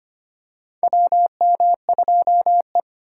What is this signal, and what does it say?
Morse code, heard as a keyed tone, about 25 wpm, 700 Hz. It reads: WM2E
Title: USA Call Signs - 4 Characters - 2x Character Spacing - 25wpm